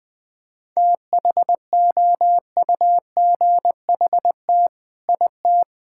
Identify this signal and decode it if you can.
Morse code, heard as a keyed tone, about 21 wpm, 700 Hz. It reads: THOUGHT IT
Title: Sets of 8 Words - 20wpm